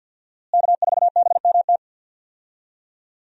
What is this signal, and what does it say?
Morse code, heard as a keyed tone, about 52 wpm, 700 Hz. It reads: K4BMT
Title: USA Call Signs - 5 Characters - Repeated 3x - 50wpm